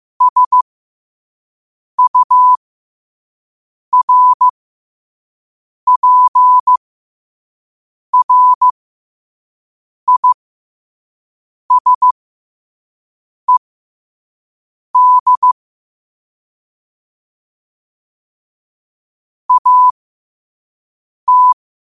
Morse code, heard as a keyed tone, about 15 wpm, spaced as for 5 wpm, 1000 Hz. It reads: SURPRISED AT
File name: qotdfiles / 5wpm